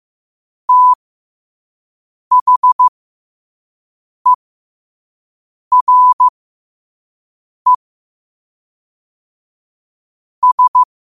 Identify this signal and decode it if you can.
Morse code, heard as a keyed tone, about 15 wpm, spaced as for 5 wpm, 1000 Hz. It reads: THERE S
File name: qotdfiles / 5wpm